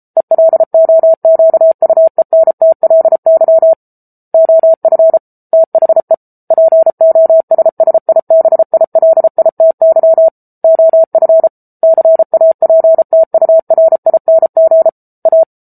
Morse code, heard as a keyed tone, about 34 wpm, 650 Hz. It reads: ELOQUENTLY OF THE POSSIBILITY OF CAPTURING A
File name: Book Two / 07-Chapter 7 - 33 WPM